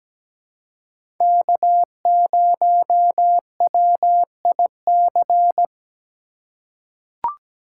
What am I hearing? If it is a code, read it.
Morse code, heard as a keyed tone, about 17 wpm, 700 Hz. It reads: K0WIC
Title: USA Call Signs - 5 Characters - 6x Character Spacing - 17wpm